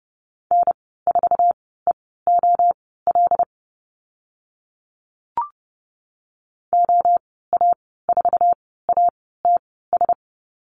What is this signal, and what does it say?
Morse code, heard as a keyed tone, about 30 wpm, spaced as for 17 wpm, 700 Hz. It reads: N4EOL OA4ATS
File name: Lesson 15g - Number 5 - Call Signs 30wpm